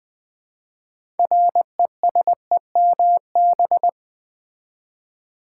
Morse code, heard as a keyed tone, about 20 wpm, 700 Hz. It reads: RESEMB
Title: Build A Word - 2 Letters At A Time - 8-Letter Words - 20wpm